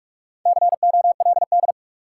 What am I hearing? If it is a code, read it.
Morse code, heard as a keyed tone, about 46 wpm, 700 Hz. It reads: COLD
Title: Sets of 5 Words - New Word Drill - 45wpm